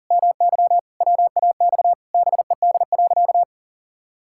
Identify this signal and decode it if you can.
Morse code, heard as a keyed tone, about 40 wpm, 700 Hz. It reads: MY WAX BED.
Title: Pangram Challenge v2 - 40wpm